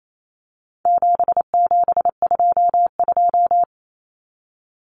Morse code, heard as a keyed tone, about 28 wpm, 700 Hz. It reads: 7722
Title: Numbers - 4-Digit 28wpm